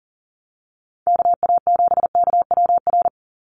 Morse code, heard as a keyed tone, about 40 wpm, 690 Hz. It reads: KA7KWR
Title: USA Call Signs 40wpm